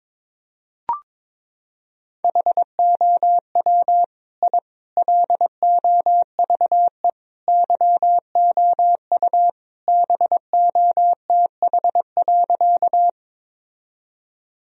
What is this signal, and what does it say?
Morse code, heard as a keyed tone, about 22 wpm, 700 Hz. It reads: HOW I LOVE YOU BOTH.
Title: Sentences from Top 1000 Words - Review Part 1 - 22wpm